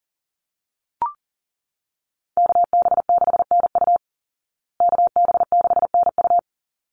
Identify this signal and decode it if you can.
Morse code, heard as a keyed tone, about 40 wpm, 700 Hz. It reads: KB6NU KB6NU
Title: Call Signs Repeated 3x - 40wpm